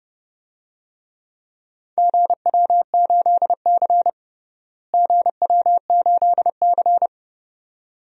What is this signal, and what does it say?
Morse code, heard as a keyed tone, about 30 wpm, 700 Hz. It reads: GW8C GW8C
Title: DXCC-100 Call Signs - 4 Characters - Repeated 2x - 30wpm